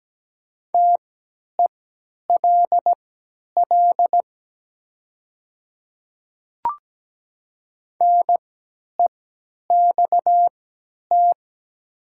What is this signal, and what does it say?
Morse code, heard as a keyed tone, about 17 wpm, spaced as for 10 wpm, 700 Hz. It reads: TELL NEXT
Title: 3x Character Spacing - Top 500 Words - 17wpm